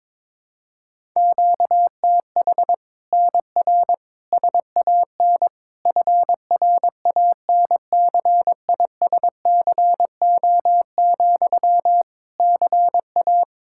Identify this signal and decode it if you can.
Morse code, heard as a keyed tone, about 22 wpm, 700 Hz. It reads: QTH NR SAN FRANCISCO, CA
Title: Practice QSO - Exchange 1 - Beginning - 22wpm